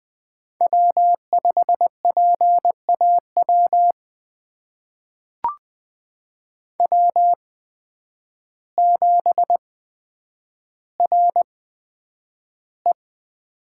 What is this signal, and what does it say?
Morse code, heard as a keyed tone, about 21 wpm, spaced as for 9 wpm, 700 Hz. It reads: W5PAW W7RE
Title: USA Call Signs - 8x Character Spacing - 20wpm